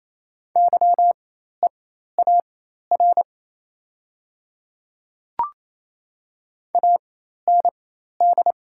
Morse code, heard as a keyed tone, about 28 wpm, spaced as for 13 wpm, 690 Hz. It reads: YEAR AND
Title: 4x Character Spacing - Top 200 Words - 28wpm